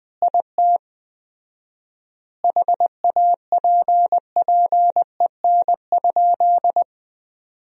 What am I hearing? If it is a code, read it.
Morse code, heard as a keyed tone, about 20 wpm, 700 Hz. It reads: IT HAPPEN?
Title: 4x Word-Spacing - Sentences from Top 400 Words - 20wpm